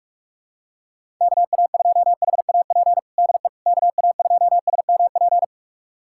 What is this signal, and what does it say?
Morse code, heard as a keyed tone, about 46 wpm, 700 Hz. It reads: KA2HAP DE KA2SMP
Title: USA Callsign DE Callsign - 6 Characters - 45wpm